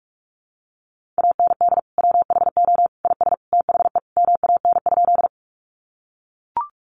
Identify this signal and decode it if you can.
Morse code, heard as a keyed tone, about 44 wpm, 700 Hz. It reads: AND WHO IS THE MAN?
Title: Sentences from Top 200 Words 45wpm